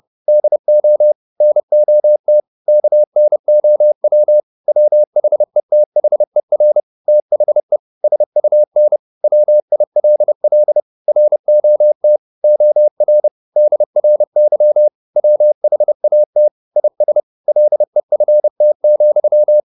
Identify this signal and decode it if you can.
Morse code, heard as a keyed tone, about 30 wpm, 600 Hz. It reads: DO NOT KNOW WHETHER THE SUN WILL ROT OR DRY WHAT IS LEFT,
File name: the old man and the sea / the old man and the sea 1 30wpm